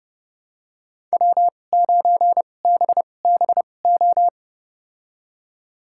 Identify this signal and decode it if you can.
Morse code, heard as a keyed tone, about 31 wpm, spaced as for 22 wpm, 700 Hz. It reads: W9BBO
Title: USA Call Signs - 5 Characters - 2x Character Spacing - 30wpm